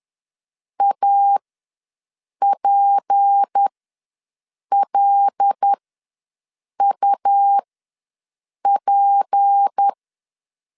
Morse code, heard as a keyed tone, about 10 wpm, spaced as for 6 wpm, 800 Hz. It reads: APLUP
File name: LEZIONI / 6 PKX0 / 6 P ws 10wpm 3751